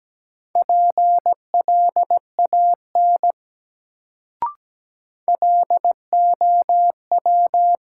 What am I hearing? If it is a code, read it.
Morse code, heard as a keyed tone, about 17 wpm, 700 Hz. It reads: PLAN LOW
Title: Sentences from Top 500 Words - New Word Drill - 17wpm